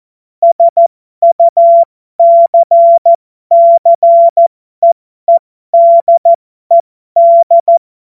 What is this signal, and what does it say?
Morse code, heard as a keyed tone, about 14 wpm, spaced as for 12 wpm, 680 Hz. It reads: SUCCEEDED
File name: Morsekurs / Witze